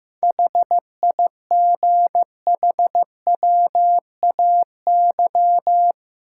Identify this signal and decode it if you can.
Morse code, heard as a keyed tone, about 15 wpm, 700 Hz. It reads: HIGHWAY
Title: Sentences from Top 1400 Words - New Word Drill - 15wpm